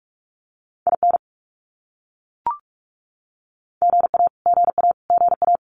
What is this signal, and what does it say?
Morse code, heard as a keyed tone, about 44 wpm, 700 Hz. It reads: IN GA GA GA
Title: US States Repeated 3x 45wpm